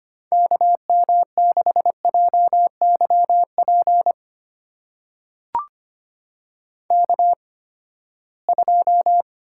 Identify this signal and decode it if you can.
Morse code, heard as a keyed tone, about 25 wpm, spaced as for 17 wpm, 700 Hz. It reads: KM6JYP K2
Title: USA Call Signs - 8x Character Spacing - 25wpm